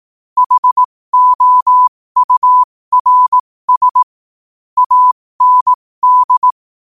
Morse code, heard as a keyed tone, about 18 wpm, spaced as for 15 wpm, 1000 Hz. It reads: HOURS AND
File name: 15wpm